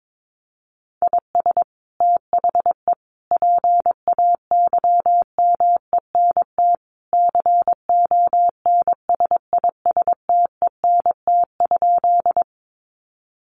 Morse code, heard as a keyed tone, about 22 wpm, 700 Hz. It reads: IS THE PAYMENT CONSISTENT?
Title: Sentences from Top 1500 Words 22wpm